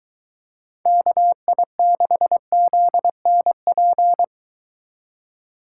Morse code, heard as a keyed tone, about 22 wpm, 700 Hz. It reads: KI6ZNP